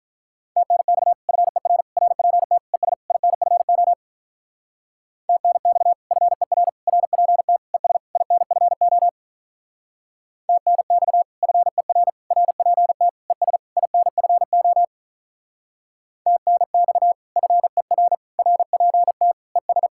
Morse code, heard as a keyed tone, about 46 wpm, 700 Hz. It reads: TNX FER RPT ES INFO TNX FER RPT ES INFO TNX FER RPT ES INFO TNX FER RPT ES